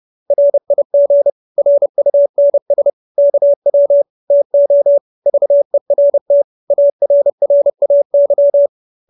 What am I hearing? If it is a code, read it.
Morse code, heard as a keyed tone, about 30 wpm, 570 Hz. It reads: RIG RUNS KW TO VERT ARRAY